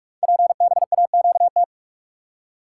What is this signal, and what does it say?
Morse code, heard as a keyed tone, about 44 wpm, 700 Hz. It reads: PBAQT
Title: Code Groups - 5 Letters - 45wpm